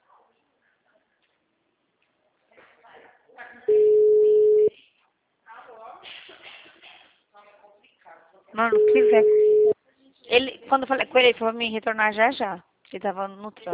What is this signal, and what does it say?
Ringback tone (Europe, 425 Hz), with speech.